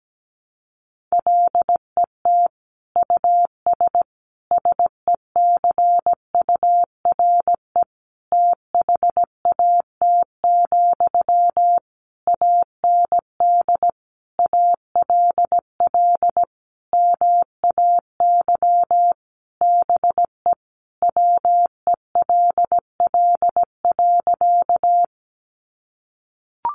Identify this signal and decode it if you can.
Morse code, heard as a keyed tone, about 17 wpm, 700 Hz. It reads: LET US SECURE THAT, AND ALL MAY BE WELL.
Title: Sentences from Top 1200 Words 17wpm